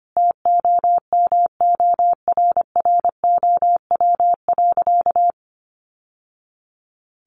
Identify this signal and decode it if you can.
Morse code, heard as a keyed tone, about 25 wpm, 700 Hz. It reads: TOMORROW.
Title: Sentences from Top 1100 Words 25wpm